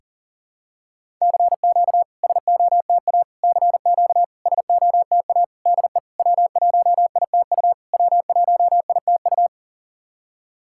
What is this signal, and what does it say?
Morse code, heard as a keyed tone, about 40 wpm, 690 Hz. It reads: CQ SOTA CQ SOTA DE W1ITU W1ITU
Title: SOTA QSO Practice 40wpm